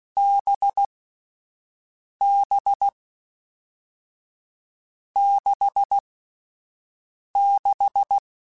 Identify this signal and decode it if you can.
Morse code, heard as a keyed tone, about 16 wpm, spaced as for 5 wpm, 800 Hz. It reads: BB 66
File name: Difficult Combos / Difficult Combos-mp3s / NDB NDB6 @5